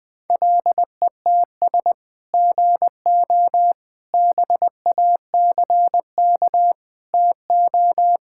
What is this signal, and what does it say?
Morse code, heard as a keyed tone, about 21 wpm, 700 Hz. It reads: LETS GO BACK TO
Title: Sentences with 3-Word Phrases - 20wpm